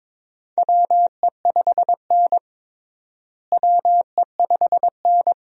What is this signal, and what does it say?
Morse code, heard as a keyed tone, about 22 wpm, 700 Hz. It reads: WE5N WE5N
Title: USA Call Signs - 4 Characters - Repeated 2x - 22wpm